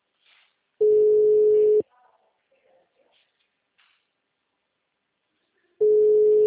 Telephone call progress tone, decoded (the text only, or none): ringback tone